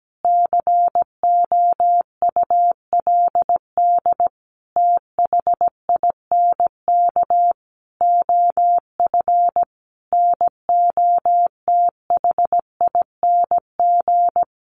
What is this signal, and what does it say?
Morse code, heard as a keyed tone, about 17 wpm, 700 Hz. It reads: COULD THINK OF NOTHING
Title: Sentences from Top 800 Words 17wpm